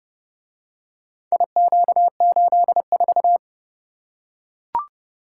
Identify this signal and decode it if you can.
Morse code, heard as a keyed tone, about 28 wpm, 700 Hz. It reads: IQ84